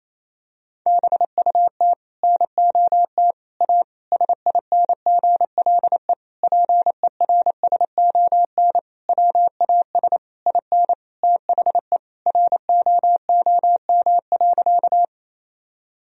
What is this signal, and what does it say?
Morse code, heard as a keyed tone, about 27 wpm, 700 Hz. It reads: BUT NOT A SINGLE PERSON WAS IN THE ROOM.
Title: Sentences from Top 1000 Words - Review Part 4 - 28wpm